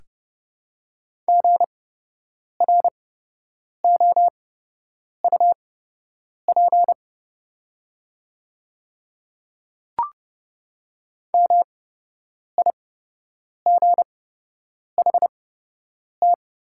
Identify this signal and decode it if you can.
Morse code, heard as a keyed tone, about 30 wpm, spaced as for 8 wpm, 690 Hz. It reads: GROUP MIGHT
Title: 8x Character Spacing - Top 500 Words - 30wpm